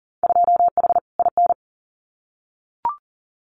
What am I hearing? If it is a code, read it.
Morse code, heard as a keyed tone, about 38 wpm, 700 Hz. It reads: 2H IN
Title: Winter Field Day Exchanges Encore 40wpm